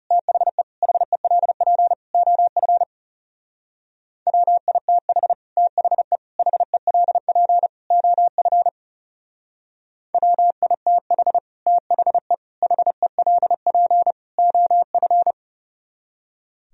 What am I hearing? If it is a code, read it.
Morse code, heard as a keyed tone, about 34 wpm, 700 Hz. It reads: THE HELP OF WITH THE HELP OF WITH THE HELP OF
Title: Speed-Racing - Sets of 4 Words 30wpm